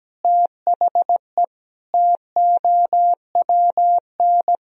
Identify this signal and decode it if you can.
Morse code, heard as a keyed tone, about 17 wpm, 700 Hz. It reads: THE TOWN
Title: Sentences from Top 1000 Words - Review Part 2 - 17wpm